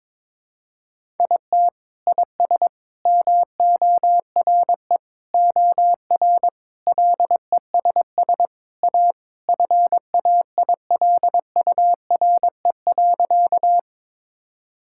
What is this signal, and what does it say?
Morse code, heard as a keyed tone, about 22 wpm, 700 Hz. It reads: IT IS MORE OR LESS A FAILURE.
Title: Sentences from Top 1000 Words - Review Part 6 - 22wpm